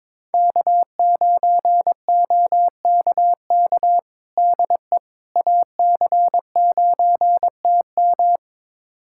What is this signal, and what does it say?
Morse code, heard as a keyed tone, about 22 wpm, 700 Hz. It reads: K9OKK DE AC9TM
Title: USA Callsign DE Callsign - 5 Characters - 22wpm